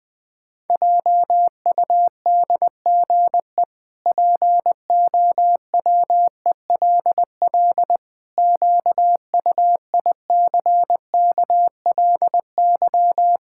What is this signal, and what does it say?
Morse code, heard as a keyed tone, about 20 wpm, 700 Hz. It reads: JUDGE POWELL QUICKLY
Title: Pangram Challenge v2 - 20wpm